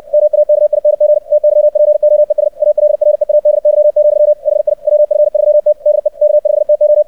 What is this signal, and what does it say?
Morse code, heard as a keyed tone, about 58 wpm, 600 Hz. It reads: MAKING TYPOES ALREADY, HI WF3T DE K5TO
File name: archived / CRSnet